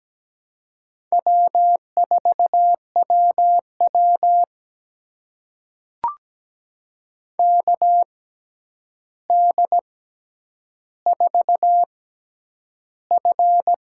Morse code, heard as a keyed tone, about 17 wpm, spaced as for 9 wpm, 700 Hz. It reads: W4WW KD4F
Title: USA Call Signs - 6x Character Spacing - 17wpm